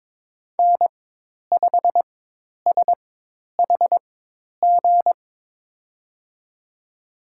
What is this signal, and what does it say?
Morse code, heard as a keyed tone, about 22 wpm, spaced as for 10 wpm, 700 Hz. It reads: N5SHG
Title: USA Call Signs - 5 Characters - 4x Character Spacing - 22wpm